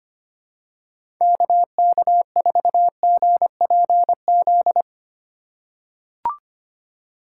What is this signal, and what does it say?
Morse code, heard as a keyed tone, about 25 wpm, 700 Hz. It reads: KK4GPZ
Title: USA Call Signs - 6 Characters - 4x Character Spacing - 25wpm